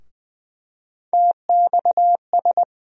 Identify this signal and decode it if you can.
Morse code, heard as a keyed tone, about 20 wpm, 700 Hz. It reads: TXS
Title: USA Call Sign Postfixes - 3 Letters 20wpm